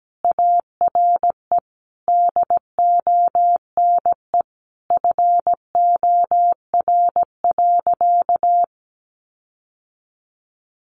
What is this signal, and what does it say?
Morse code, heard as a keyed tone, about 17 wpm, 700 Hz. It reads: ARE DONE FOR.
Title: Sentences from Top 1500 Words 17wpm